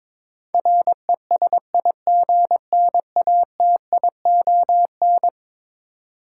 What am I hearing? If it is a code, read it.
Morse code, heard as a keyed tone, about 22 wpm, 700 Hz. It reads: RESIGNATION
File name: practice / 22wpm / Eleven Letter Words - 22wpm